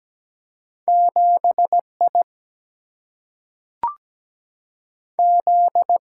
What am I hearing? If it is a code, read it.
Morse code, heard as a keyed tone, about 17 wpm, 700 Hz. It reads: 7 I Z
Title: Single-Letter Single-Number 17wpm